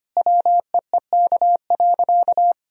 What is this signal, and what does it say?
Morse code, heard as a keyed tone, about 25 wpm, 700 Hz. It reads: WEEK.
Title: Sentences from Top 1000 Words - Review Part 2 - 25wpm